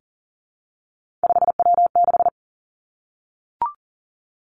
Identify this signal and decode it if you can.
Morse code, heard as a keyed tone, about 39 wpm, 700 Hz. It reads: 5W6
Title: Three Challenging Letter-Number Combinations 40wpm